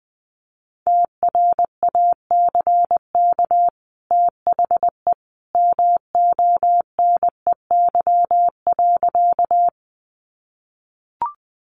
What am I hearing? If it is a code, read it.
Morse code, heard as a keyed tone, about 20 wpm, 700 Hz. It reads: TRACK THE MONEY.
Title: Sentences from Top 1100 Words 20wpm